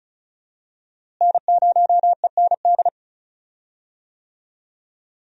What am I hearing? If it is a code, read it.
Morse code, heard as a keyed tone, about 34 wpm, 700 Hz. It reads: N0END
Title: USA Call Signs - 5 Characters - Repeated 3x - 35wpm